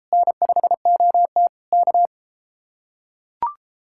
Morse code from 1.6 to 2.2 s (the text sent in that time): K